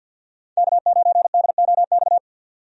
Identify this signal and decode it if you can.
Morse code, heard as a keyed tone, about 51 wpm, 700 Hz. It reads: K9DOX
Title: USA Call Signs - 5 Characters - 50wpm